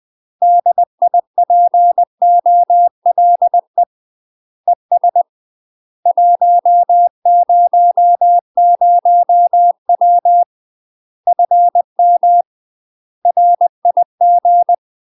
Morse code, heard as a keyed tone, about 20 wpm, 700 Hz. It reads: DIPOLE ES 100W FM RIG
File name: UK MASTER QSOs TRAINING / 20WPM / UK MASTER QSOs 0029